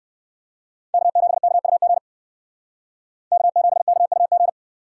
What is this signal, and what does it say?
Morse code, heard as a keyed tone, about 56 wpm, 700 Hz. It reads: K7KUG K7KUG
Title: Speed Racing - USA 1x3 Call Signs 45wpm